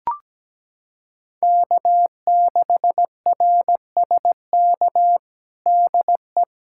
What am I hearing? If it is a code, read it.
Morse code, heard as a keyed tone, about 17 wpm, 700 Hz. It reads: K6RSK DE
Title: USA Callsign DE Callsign - 5 Characters - 17wpm